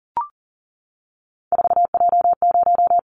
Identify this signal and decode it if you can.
Morse code, heard as a keyed tone, about 38 wpm, 690 Hz. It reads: 4J0
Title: Callsign Prefixes for Top 100 DXCC - 40wpm